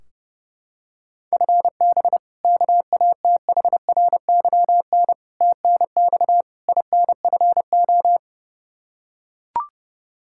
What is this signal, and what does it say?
Morse code, heard as a keyed tone, about 30 wpm, 700 Hz. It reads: FB KATHRYN TNX INFO
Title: Practice QSO - Exchange 3 - More Info - 30wpm